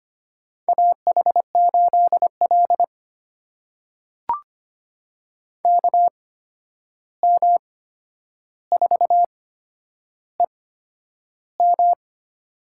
Morse code, heard as a keyed tone, about 25 wpm, spaced as for 10 wpm, 700 Hz. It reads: AH8L KM4EM
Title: USA Call Signs - 8x Character Spacing - 25wpm